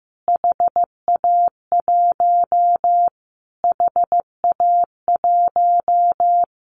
Morse code, heard as a keyed tone, about 15 wpm, 700 Hz. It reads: HA1 HA1
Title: Callsign Prefixes for Top 25 DXCC - Repeated 3x - 15wpm